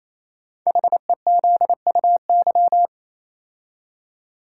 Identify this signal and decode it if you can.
Morse code, heard as a keyed tone, about 29 wpm, 700 Hz. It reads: HEZUY
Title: Code Groups - 5 Letters - 28wpm